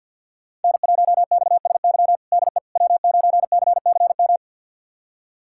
Morse code, heard as a keyed tone, about 49 wpm, 700 Hz. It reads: N1XSY DE W9CCM
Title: USA Callsign DE Callsign - 5 Characters - 50wpm